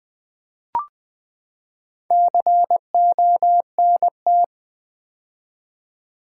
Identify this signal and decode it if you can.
Morse code, heard as a keyed tone, about 20 wpm, 700 Hz. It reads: CONT
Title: Build A Word - 4 Letters At A Time - 11-Letter Words - 20wpm